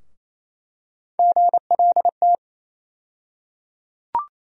Morse code, heard as a keyed tone, about 28 wpm, 700 Hz. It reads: GLT
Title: USA Call Sign Postfixes - 3 Letters 28wpm